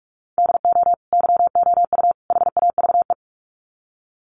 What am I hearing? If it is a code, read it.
Morse code, heard as a keyed tone, about 44 wpm, 700 Hz. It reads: DO YOU HAVE
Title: Sets of 3 Words 45wpm